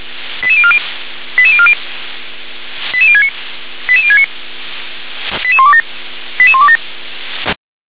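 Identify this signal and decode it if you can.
Five-tone selective call (ZVEI), heard as quick runs of five tones, tones of about 80 ms. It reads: 88040 88040 88069 88069 88127 88127
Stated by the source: ZVEI five-tone selective call received with an SDR.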